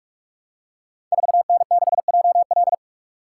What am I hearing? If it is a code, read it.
Morse code, heard as a keyed tone, about 44 wpm, 690 Hz. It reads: 4N6JL
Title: Code Groups - 5 Letter:Numbers - 45wpm